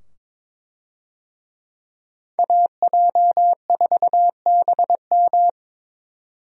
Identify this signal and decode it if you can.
Morse code, heard as a keyed tone, about 22 wpm, 700 Hz. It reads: AJ4BM
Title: USA Call Signs - 5 Characters - 22wpm